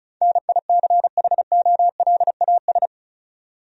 Morse code, heard as a keyed tone, about 36 wpm, 700 Hz. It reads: NICHOLAS
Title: USA Common Names - 35wpm